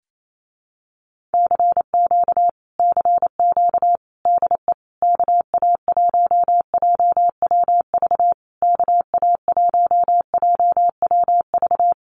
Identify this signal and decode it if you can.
Morse code, heard as a keyed tone, about 28 wpm, 700 Hz. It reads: CQ CQ DE KA1JWV KA1JWV
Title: Common QSO Phrases Encore 28wpm